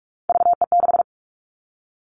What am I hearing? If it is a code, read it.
Morse code, heard as a keyed tone, about 44 wpm, 700 Hz. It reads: VE6
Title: All Cut Numbers - 3-Digit 45wpm